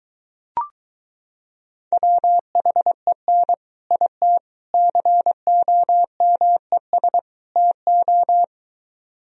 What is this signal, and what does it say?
Morse code, heard as a keyed tone, about 23 wpm, 700 Hz. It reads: WHEN IT COMES TO